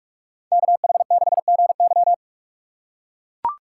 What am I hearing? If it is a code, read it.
Morse code, heard as a keyed tone, about 44 wpm, 700 Hz. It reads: KH6GY